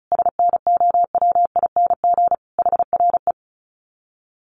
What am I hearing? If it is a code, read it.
Morse code, heard as a keyed tone, about 34 wpm, 700 Hz. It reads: SNOWING HRE
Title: Common QSO Phrases 35wpm